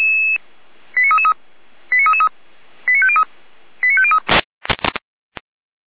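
Five-tone selective call (ZVEI), heard as quick runs of five tones, tones of about 70 ms. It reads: 89333 89333 89503 89503
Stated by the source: ZVEI five-tone selective call 89501 received on SDR.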